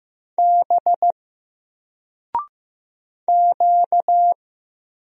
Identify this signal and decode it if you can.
Morse code, heard as a keyed tone, about 15 wpm, 700 Hz. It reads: B Q
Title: Single Letter-Number 15wpm